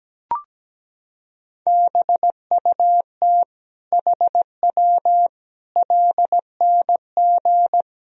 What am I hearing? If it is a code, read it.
Morse code, heard as a keyed tone, about 17 wpm, 700 Hz. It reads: BUT HW LNG
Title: Sentences from Top 500 Words and Abbreviations 17wpm